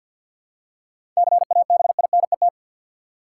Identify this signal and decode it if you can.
Morse code, heard as a keyed tone, about 47 wpm, 700 Hz. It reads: CABINET
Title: Seven Letter Words - 50wpm